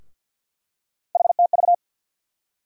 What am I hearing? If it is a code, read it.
Morse code, heard as a keyed tone, about 53 wpm, 700 Hz. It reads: HTV
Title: USA Call Sign Postfixes - 3 Letters 50wpm